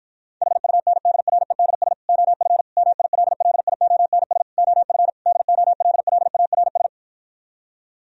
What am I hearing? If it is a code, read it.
Morse code, heard as a keyed tone, about 52 wpm, 700 Hz. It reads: HUNDREDS OF MILLIONS OF DOLLARS